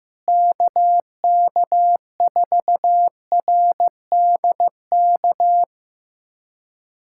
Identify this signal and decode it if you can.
Morse code, heard as a keyed tone, about 16 wpm, 700 Hz. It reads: KK4RDK
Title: USA Callsign DE Callsign 15wpm